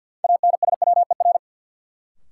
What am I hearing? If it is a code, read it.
Morse code, heard as a keyed tone, about 53 wpm, 700 Hz. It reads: ANSWER
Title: Sentences from Top 300 Words - New Word Drill - 50wpm